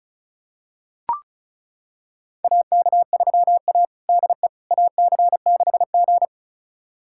Morse code, heard as a keyed tone, about 36 wpm, 700 Hz. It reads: AK3A DE AC6G
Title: USA Callsign DE Callsign - 4 Characters - 35wpm